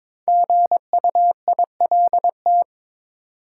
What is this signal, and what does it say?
Morse code, heard as a keyed tone, about 22 wpm, 700 Hz. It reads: GUILT
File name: 5 Letter Words - First 3 Letters Given - 22wpm